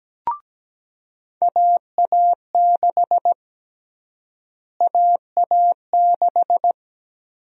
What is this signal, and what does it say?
Morse code, heard as a keyed tone, about 17 wpm, 700 Hz. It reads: AA6 AA6
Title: USA Call Sign Prefixes - Repeated 2x - 17wpm